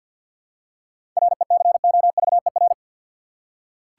Morse code, heard as a keyed tone, about 46 wpm, 700 Hz. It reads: RECOVER
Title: Seven Letter Words - 50wpm